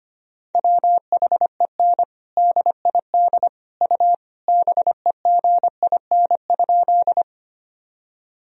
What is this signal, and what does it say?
Morse code, heard as a keyed tone, about 25 wpm, 700 Hz. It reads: WHEN DID U BEGIN?
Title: Sentences from Top 300 Words and Abbreviations 25wpm